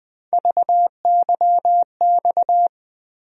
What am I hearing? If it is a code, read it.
Morse code, heard as a keyed tone, about 20 wpm, 700 Hz. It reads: VYX
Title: USA Call Sign Postfixes - 3 Letters 20wpm